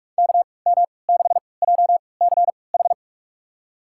Morse code, heard as a keyed tone, about 45 wpm, spaced as for 25 wpm, 700 Hz. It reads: KM6JCH
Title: USA Call Signs - 6 Characters - 3x Character Spacing - 45wpm